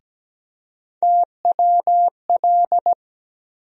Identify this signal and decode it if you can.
Morse code, heard as a keyed tone, about 17 wpm, 700 Hz. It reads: TWL
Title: USA Call Sign Postfixes - 3 Letters 17wpm